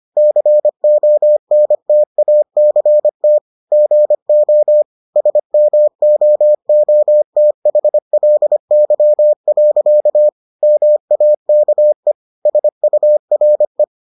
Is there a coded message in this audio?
Yes: Morse code from 0.2 to 13.8 s, heard as a keyed tone, about 25 wpm, 600 Hz. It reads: CONTACT GO SMOOTHLY. MAKE SURE